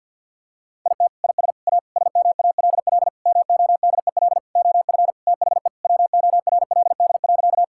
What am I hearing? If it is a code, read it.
Morse code, heard as a keyed tone, about 50 wpm, 700 Hz. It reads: IT IS A SMALL MODEL OF THE WORLD.
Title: Sentences from Top 1000 Words - Review Part 5 - 50wpm